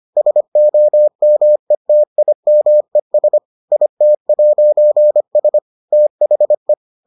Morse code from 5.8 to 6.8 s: THE

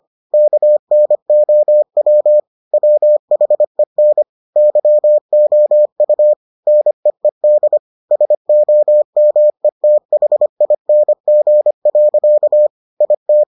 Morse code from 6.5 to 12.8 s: NEED SOMETHING.

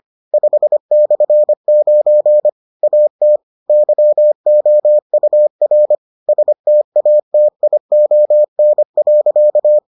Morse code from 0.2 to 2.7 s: 5/9

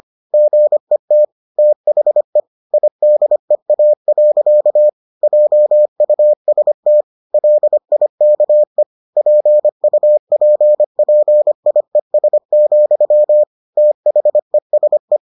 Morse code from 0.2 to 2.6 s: GET THE

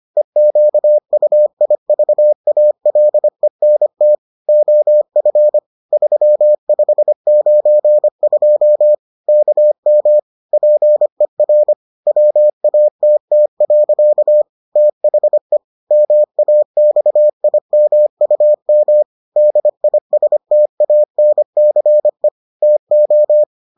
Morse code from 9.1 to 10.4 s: KM

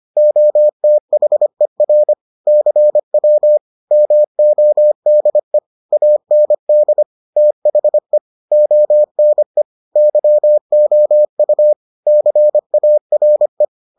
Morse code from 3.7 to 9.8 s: MODE AND THE ONE

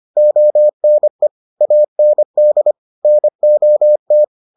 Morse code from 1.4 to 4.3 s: AND NOT

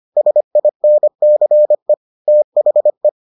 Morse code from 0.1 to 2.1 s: SINCE

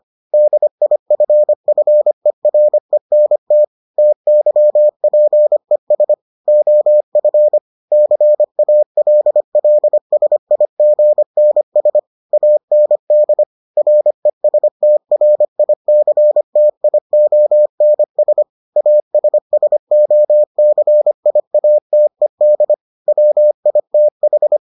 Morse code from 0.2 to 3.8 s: DIFFERENT

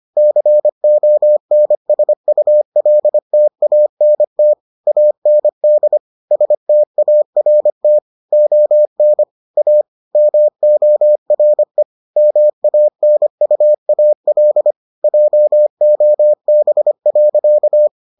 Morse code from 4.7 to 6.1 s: AND

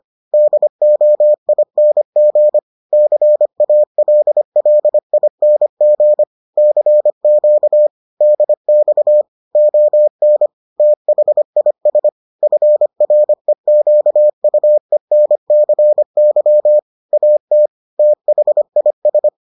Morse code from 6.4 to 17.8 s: CQ DX ON THIS FREQUENCY AT

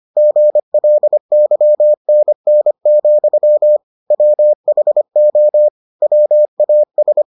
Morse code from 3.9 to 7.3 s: WHO WAS

